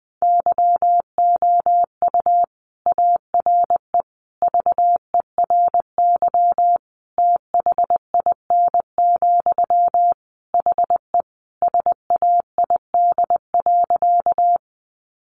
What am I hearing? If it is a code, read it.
Morse code, heard as a keyed tone, about 19 wpm, 700 Hz. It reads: YOU ARE VERY THIN, HE SAID.
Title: Sentences from Top 1200 Words 20wpm